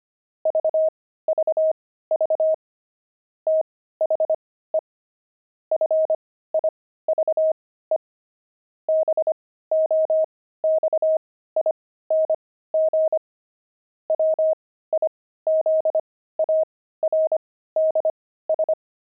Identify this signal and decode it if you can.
Morse code, heard as a keyed tone, about 25 wpm, spaced as for 15 wpm, 630 Hz. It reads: VVV THE FIVE BOXING WIZARDS
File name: pangram-fs-15-cs-25-jumping